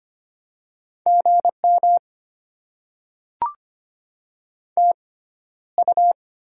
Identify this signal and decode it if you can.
Morse code, heard as a keyed tone, about 25 wpm, spaced as for 13 wpm, 700 Hz. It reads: GM TU
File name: Top 20 QSO Elements - 6x Character Spacing - 25wpm